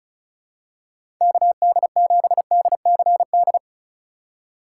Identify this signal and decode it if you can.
Morse code, heard as a keyed tone, about 36 wpm, 700 Hz. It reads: KD7DCD
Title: USA Call Signs - 6 Characters - 2x Character Spacing - 35wpm